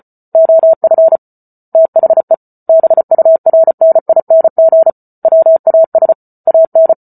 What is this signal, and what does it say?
Morse code, heard as a keyed tone, about 34 wpm, 650 Hz. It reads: OF THE BURNING WAS AN